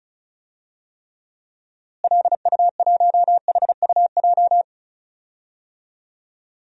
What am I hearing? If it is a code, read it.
Morse code, heard as a keyed tone, about 36 wpm, 700 Hz. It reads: LU1HUJ
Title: DXCC-100 Call Signs - 6 Characters - Repeated 3x - 35wpm